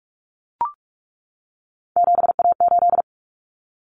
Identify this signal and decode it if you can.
Morse code, heard as a keyed tone, about 44 wpm, 700 Hz. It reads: 7A8
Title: Common Cut Numbers - 3-Digit 45wpm